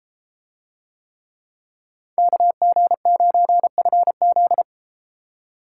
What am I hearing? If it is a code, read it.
Morse code, heard as a keyed tone, about 32 wpm, 700 Hz. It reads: KG9FZ